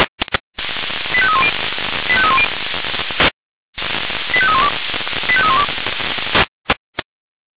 Five-tone selective call (ZVEI), heard as quick runs of five tones, tones of about 70 ms. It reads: 85310 85310 85312 85312
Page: SDR/ZVEI